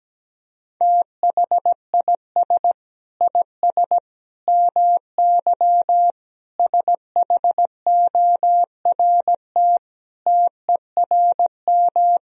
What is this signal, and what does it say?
Morse code, heard as a keyed tone, about 17 wpm, 700 Hz. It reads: THIS IS MY SHORT TERM
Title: Sentences from Top 1000 Words - Review Part 4 - 17wpm